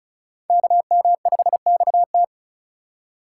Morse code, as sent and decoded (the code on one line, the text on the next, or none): -.- -- ..... -..- -
KM5XT